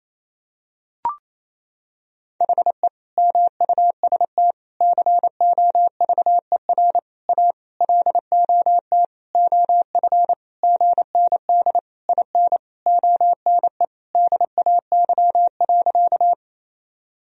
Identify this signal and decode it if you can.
Morse code, heard as a keyed tone, about 28 wpm, 700 Hz. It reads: HE MUST COVER A LOT OF GND IN ONE DAY.
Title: Sentences from Top 500 Words and Abbreviations 28wpm